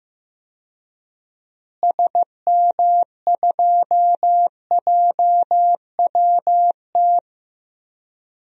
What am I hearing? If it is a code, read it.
Morse code, heard as a keyed tone, about 15 wpm, 700 Hz. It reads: SM2JWT
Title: DXCC-100 Call Signs - 6 Characters - Repeated 2x - 15wpm